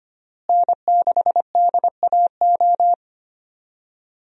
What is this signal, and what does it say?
Morse code, heard as a keyed tone, about 25 wpm, 700 Hz. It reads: N6DAO
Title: USA Call Signs - 5 Characters - Repeated 2x - 25wpm